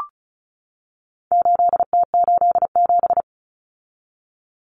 Morse code, heard as a keyed tone, about 34 wpm, 700 Hz. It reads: 8T87
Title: Common Cut Numbers - 4-Digit 35wpm